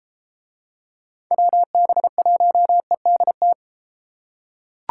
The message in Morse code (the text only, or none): WB1EDT